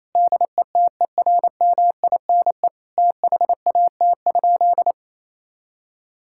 Morse code, heard as a keyed tone, about 28 wpm, 700 Hz. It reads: DETERMINE THAT?
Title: Sentences from Top 1000 Words - Review Part 5 - 28wpm